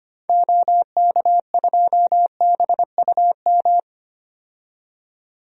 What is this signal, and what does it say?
Morse code, heard as a keyed tone, about 25 wpm, 700 Hz. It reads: OK2BUM
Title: DXCC-100 Call Signs - 6 Characters - 25wpm